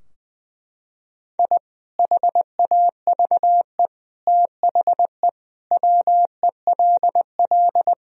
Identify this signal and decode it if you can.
Morse code, heard as a keyed tone, about 20 wpm, 700 Hz. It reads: I HAVE THE WELL
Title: Sentences from Top 1000 Words - Review Part 3 - 20wpm